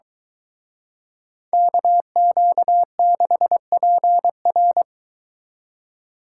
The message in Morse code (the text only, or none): KQ6PR